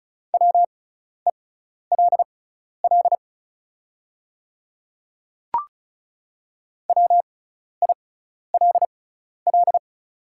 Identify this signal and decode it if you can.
Morse code, heard as a keyed tone, about 36 wpm, spaced as for 12 wpm, 700 Hz. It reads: WELL WILL
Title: 6x Character Spacing - Top 200 Words - 35wpm